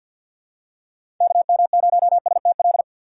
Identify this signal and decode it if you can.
Morse code, heard as a keyed tone, about 53 wpm, 700 Hz. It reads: KM0STL
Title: USA Call Signs - 6 Characters - 8x Character Spacing - 50wpm